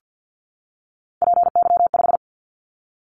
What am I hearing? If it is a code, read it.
Morse code, heard as a keyed tone, about 50 wpm, 700 Hz. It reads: PY5
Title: Callsign Prefixes for Top 100 DXCC - 50wpm